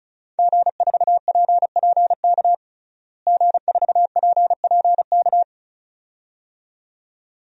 Morse code, heard as a keyed tone, about 36 wpm, 700 Hz. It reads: G4PPK G4PPK
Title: DXCC-100 Call Signs - 5 Characters - Repeated 2x - 35wpm